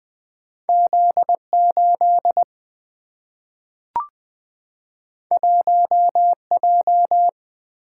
Morse code, heard as a keyed tone, about 20 wpm, 700 Hz. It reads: Z8 1J
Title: Single-Letter Single-Number 20wpm